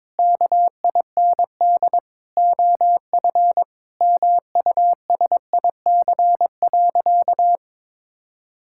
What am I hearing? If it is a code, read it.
Morse code, heard as a keyed tone, about 22 wpm, 700 Hz. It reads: KIND OF MUSIC.
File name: practice / 22wpm / Sentences with 3-Word Phrases - 22wpm